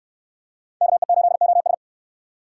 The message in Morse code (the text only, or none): KE8OS